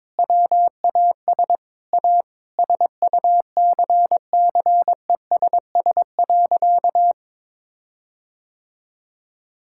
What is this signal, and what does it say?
Morse code, heard as a keyed tone, about 22 wpm, 700 Hz. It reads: WAS A SUCCESS.
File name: practice / 22wpm / Sentences with 8-Word Phrases - 22wpm